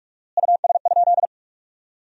Morse code, heard as a keyed tone, about 47 wpm, 700 Hz. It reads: US?
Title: 4x Word-Spacing - Sentences from Top 200 Words - 45wpm